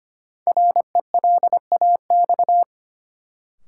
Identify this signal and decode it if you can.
Morse code, heard as a keyed tone, about 25 wpm, 700 Hz. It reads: RELAX
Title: Sentences from Top 1400 Words - New Word Drill - 25wpm